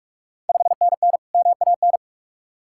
Morse code, heard as a keyed tone, about 45 wpm, 690 Hz. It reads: 5NN MAN
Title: Ontario QSO Party Exchange 45wpm